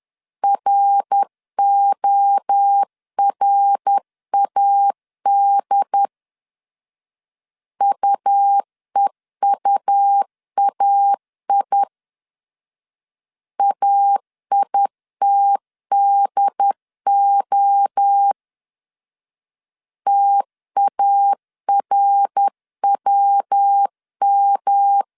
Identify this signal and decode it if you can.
Morse code, heard as a keyed tone, about 10 wpm, 800 Hz. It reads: RORAD UEUAI AITDO TARWM
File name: CW / DET / 1 3 1500C S B 10wpm 3335